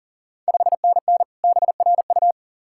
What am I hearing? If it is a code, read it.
Morse code, heard as a keyed tone, about 39 wpm, 690 Hz. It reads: 5NN BRU
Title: Ontario QSO Party Exchange 40wpm